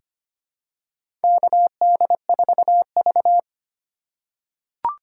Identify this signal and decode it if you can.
Morse code, heard as a keyed tone, about 25 wpm, 700 Hz. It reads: KD4V
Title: USA Call Signs - 4 Characters - 3x Character Spacing - 25wpm